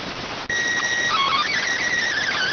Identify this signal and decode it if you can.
SSTV (slow-scan television), PD 290.